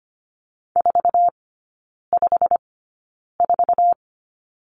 Morse code, heard as a keyed tone, about 25 wpm, spaced as for 9 wpm, 690 Hz. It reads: 454
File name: Lesson 15e - Number 5 - Three Characters 25wpm